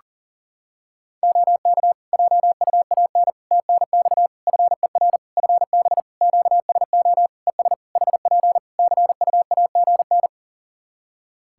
Morse code, heard as a keyed tone, about 40 wpm, 700 Hz. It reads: OK JUAN TNX FER FB QSO ES HP CUAGN
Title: K1USN SST QSO Practice 40wpm